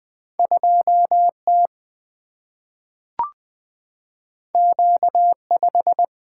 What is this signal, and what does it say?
Morse code, heard as a keyed tone, about 20 wpm, 700 Hz. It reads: 2T Q5
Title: Single-Letter Single-Number 20wpm